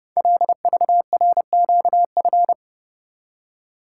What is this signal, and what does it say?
Morse code, heard as a keyed tone, about 28 wpm, 700 Hz. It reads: LVRQF
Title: Code Groups - 5 Letters - 30wpm